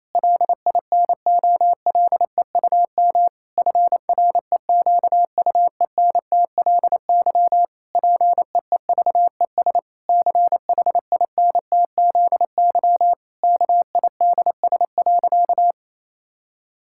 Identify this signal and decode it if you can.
Morse code, heard as a keyed tone, about 28 wpm, 700 Hz. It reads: LINOLEUM FREQUENTLY PEEVES CHINTZY KIDS.
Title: Pangram Challenge v2 - 28wpm